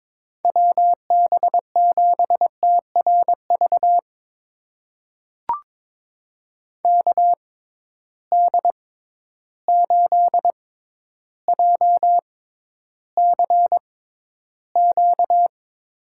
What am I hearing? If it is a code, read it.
Morse code, heard as a keyed tone, about 23 wpm, spaced as for 11 wpm, 700 Hz. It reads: WB7TRV KD8JCQ
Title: USA Call Signs - 6 Characters - 6x Character Spacing - 22wpm